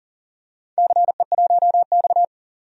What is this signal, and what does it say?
Morse code, heard as a keyed tone, about 42 wpm, 700 Hz. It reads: CE1X
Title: DXCC-100 Call Signs - 4 Characters - 40wpm